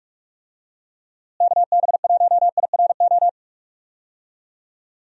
Morse code, heard as a keyed tone, about 46 wpm, 700 Hz. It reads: KB1IRO
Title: USA Call Signs - 6 Characters - Repeated 3x - 45wpm